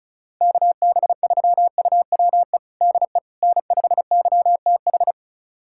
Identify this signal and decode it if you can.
Morse code, heard as a keyed tone, about 34 wpm, 700 Hz. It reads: KB3UWE DE N5YTH
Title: USA Callsign DE Callsign 35wpm